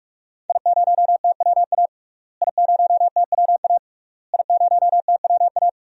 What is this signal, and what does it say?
Morse code, heard as a keyed tone, about 44 wpm, 700 Hz. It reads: I0TWA I0TWA I0TWA
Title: DXCC-100 Call Signs - 5 Characters - Repeated 3x - 45wpm